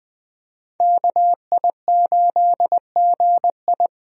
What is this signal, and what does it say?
Morse code, heard as a keyed tone, about 20 wpm, 700 Hz. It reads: KI8GI
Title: USA Call Signs - 5 Characters - Repeated 2x - 20wpm